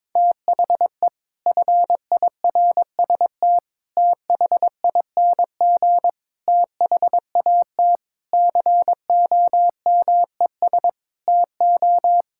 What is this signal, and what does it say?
Morse code, heard as a keyed tone, about 22 wpm, 700 Hz. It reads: THE FIRST THING THAT COMES TO
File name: practice / 22wpm / Sets of 8 Words - 22wpm